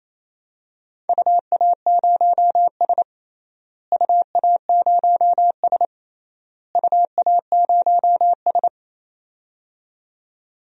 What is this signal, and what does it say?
Morse code, heard as a keyed tone, about 28 wpm, 700 Hz. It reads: UA0S UA0S UA0S
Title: DXCC-100 Call Signs - 4 Characters - Repeated 3x - 28wpm